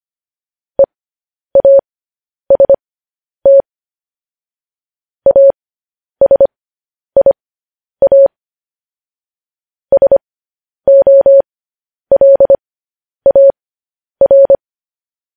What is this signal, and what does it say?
Morse code, heard as a keyed tone, about 25 wpm, spaced as for 10 wpm, 550 Hz. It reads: EAST ASIA SOLAR
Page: News Headlines in Morse Code at 10 WPM